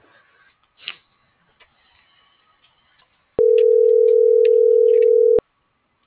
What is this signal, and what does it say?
Ringback tone (North America), 440 + 480 Hz.